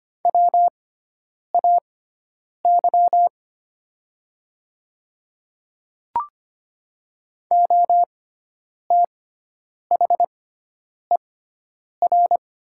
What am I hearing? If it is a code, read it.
Morse code, heard as a keyed tone, about 25 wpm, spaced as for 9 wpm, 700 Hz. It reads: WAY OTHER
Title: 6x Character Spacing - Top 100 Words - 25wpm